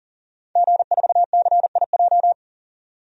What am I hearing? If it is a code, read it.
Morse code, heard as a keyed tone, about 40 wpm, 700 Hz. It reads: G4CIJ
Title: DXCC-100 Call Signs - 5 Characters - 40wpm